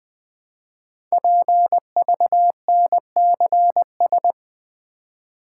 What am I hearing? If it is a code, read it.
Morse code, heard as a keyed tone, about 20 wpm, 700 Hz. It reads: PVNCS
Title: Code Groups - 5 Letter:Numbers - 20wpm